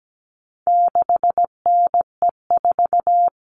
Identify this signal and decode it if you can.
Morse code, heard as a keyed tone, about 17 wpm, 700 Hz. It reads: 6NE4
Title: All Cut Numbers - 4-Digit 17wpm